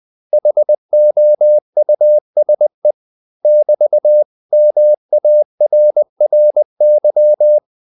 Morse code, heard as a keyed tone, about 20 wpm, 600 Hz. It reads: HOUSE <BT>MARRY
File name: CW / Practice / 500MostCommonEnglishWords20